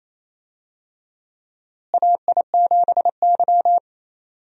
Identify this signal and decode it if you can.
Morse code, heard as a keyed tone, about 28 wpm, 700 Hz. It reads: AI7Y
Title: USA Call Signs - 4 Characters - Repeated 3x - 28wpm